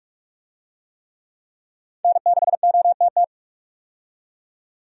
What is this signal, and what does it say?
Morse code, heard as a keyed tone, about 46 wpm, 700 Hz. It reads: N6OTT
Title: USA Call Signs - 5 Characters - 45wpm